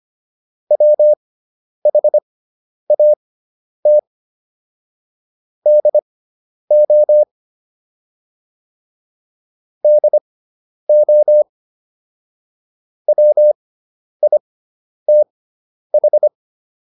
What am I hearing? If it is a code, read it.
Morse code, heard as a keyed tone, about 25 wpm, spaced as for 10 wpm, 600 Hz. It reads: WHAT DO DO WITH